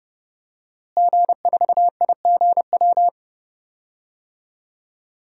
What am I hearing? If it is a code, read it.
Morse code, heard as a keyed tone, about 30 wpm, 700 Hz. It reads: G4IGW
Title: DXCC-100 Call Signs - 5 Characters - Repeated 2x - 30wpm